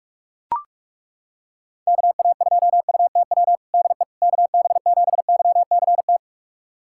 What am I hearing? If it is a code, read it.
Morse code, heard as a keyed tone, about 42 wpm, 700 Hz. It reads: KA2UTW DE KB7YCT
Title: USA Callsign DE Callsign 45wpm